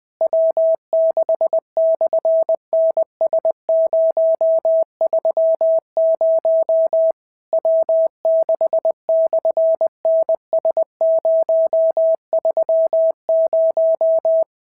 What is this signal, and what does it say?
Morse code, heard as a keyed tone, about 20 wpm, 650 Hz. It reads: W6/NS030 W6/NS030
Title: SOTA QSO Practice 20wpm